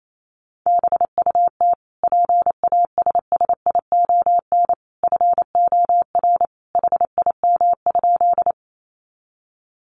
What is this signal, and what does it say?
Morse code, heard as a keyed tone, about 28 wpm, 700 Hz. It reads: BUT PASSION FOR HIM?
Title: Sentences from Top 1300 Words 28wpm